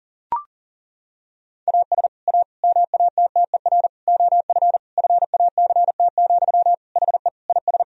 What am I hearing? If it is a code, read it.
Morse code, heard as a keyed tone, about 37 wpm, 700 Hz. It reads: AS A MATTER OF FACT, HE IS
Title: Sentences from Top 1000 Words - Review Part 3 - 40wpm